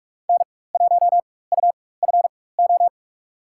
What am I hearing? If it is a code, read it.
Morse code, heard as a keyed tone, about 44 wpm, spaced as for 21 wpm, 700 Hz. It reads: N1UFO
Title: USA Call Signs - 5 Characters - 4x Character Spacing - 45wpm